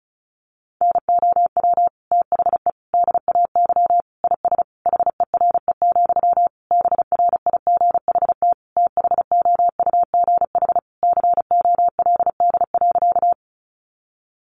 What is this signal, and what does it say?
Morse code, heard as a keyed tone, about 34 wpm, 700 Hz. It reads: NOW THE DAY IS HERE, BRIGHT THOUGH COLD.
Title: Sentences from Top 1000 Words 35wpm